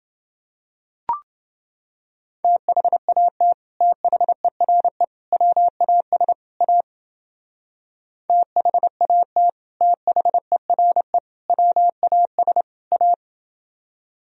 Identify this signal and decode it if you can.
Morse code, heard as a keyed tone, about 30 wpm, 700 Hz. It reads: THAT THERE WAS A THAT THERE WAS A